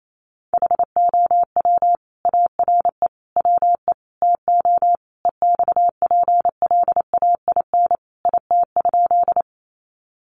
Morse code, heard as a keyed tone, about 28 wpm, 700 Hz. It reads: HOW ARE WE TO EXPLAIN IT?
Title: Sentences from Top 1100 Words 28wpm